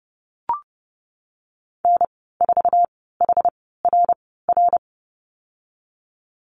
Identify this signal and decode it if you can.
Morse code, heard as a keyed tone, about 30 wpm, spaced as for 17 wpm, 700 Hz. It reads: N4HRR
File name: Lesson 19g - Letter w - Call Signs 30wpm